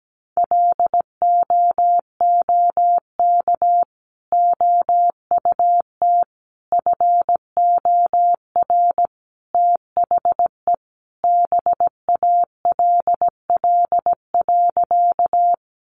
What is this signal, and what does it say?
Morse code, heard as a keyed tone, about 17 wpm, 700 Hz. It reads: LOOK OUT FOR THE BALL.
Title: Sentences from Top 900 Words 17wpm